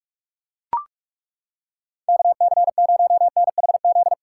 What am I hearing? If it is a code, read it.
Morse code, heard as a keyed tone, about 41 wpm, 700 Hz. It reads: KC0NHZ
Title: USA Callsign DE Callsign - 6 Characters - 45wpm